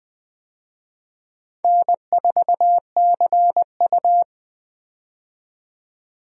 Morse code, heard as a keyed tone, about 20 wpm, 700 Hz. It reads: N4CU
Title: USA Call Signs - 4 Characters - 20wpm